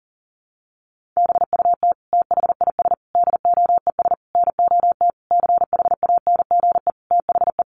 Morse code, heard as a keyed tone, about 40 wpm, 700 Hz. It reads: BUT THIS DOES NOT CHANGE THE
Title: Sentences from Top 1400 Words 40wpm